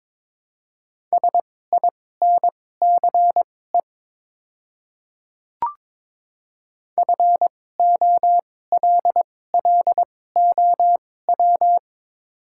Morse code, heard as a keyed tone, about 22 wpm, spaced as for 16 wpm, 700 Hz. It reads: SINCE FOLLOW
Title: 2x Character Spacing - Top 500 Words - 22wpm